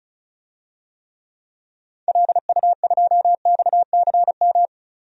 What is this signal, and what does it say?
Morse code, heard as a keyed tone, about 34 wpm, 700 Hz. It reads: LU2XCM
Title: DXCC-100 Call Signs - 6 Characters - 35wpm